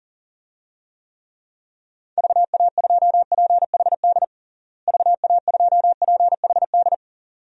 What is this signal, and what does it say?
Morse code, heard as a keyed tone, about 40 wpm, 700 Hz. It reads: VA2PHD VA2PHD
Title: DXCC-100 Call Signs - 6 Characters - Repeated 2x - 40wpm